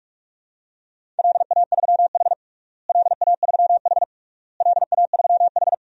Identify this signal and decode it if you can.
Morse code, heard as a keyed tone, about 46 wpm, 700 Hz. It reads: LA3H LA3H LA3H
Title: DXCC-100 Call Signs - 4 Characters - Repeated 3x - 45wpm